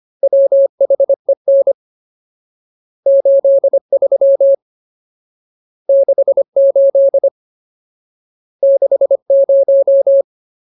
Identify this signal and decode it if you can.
Morse code, heard as a keyed tone, about 25 wpm, 550 Hz. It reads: WHEN 83 68 60
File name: Ultimate Word List / 25WPM Ultimate